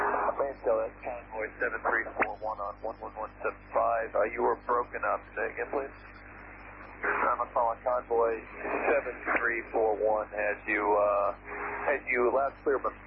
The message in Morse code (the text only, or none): none